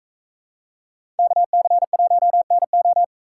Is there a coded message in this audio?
Yes: Morse code from 1.2 to 3.0 s, heard as a keyed tone, about 42 wpm, 700 Hz. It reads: KC1NO